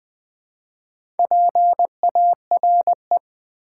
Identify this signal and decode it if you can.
Morse code, heard as a keyed tone, about 20 wpm, 700 Hz. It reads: PARE
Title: Build A Word - 4 Letters At A Time - 8-Letter Words - 20wpm